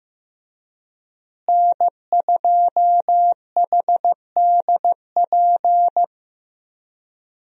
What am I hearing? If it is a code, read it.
Morse code, heard as a keyed tone, about 14 wpm, 700 Hz. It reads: N2HDP
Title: USA Call Signs - 5 Characters - Repeated 2x - 15wpm